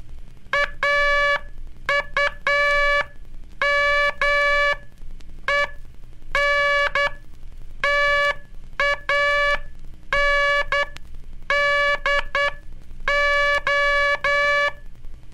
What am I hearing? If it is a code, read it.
Morse code, heard as a keyed tone, about 10 wpm, spaced as for 8 wpm, 1700 Hz, with noise. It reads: AUMENTANDO
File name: cursos / Curso CW mp3 / CW Aula - 15